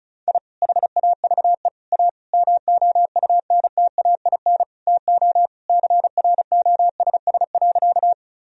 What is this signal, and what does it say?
Morse code, heard as a keyed tone, about 34 wpm, 700 Hz. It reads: I HAVE A MOUNTAIN TO CROSS.
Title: Sentences from Top 1000 Words - Review Part 1 - 35wpm